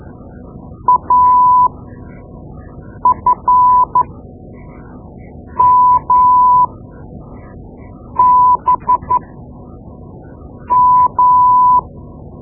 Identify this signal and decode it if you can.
Morse code, heard as a keyed tone, about 13 wpm, spaced as for 5 wpm, 1000 Hz. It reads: AFMBM
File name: Morse Lessons / Morse Lessons / Tape-2B